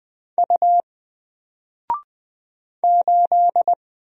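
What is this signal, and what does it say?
Morse code, heard as a keyed tone, about 20 wpm, 700 Hz. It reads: U 8
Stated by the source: Single letter and number Morse drill at 20 wpm, from N4LCD.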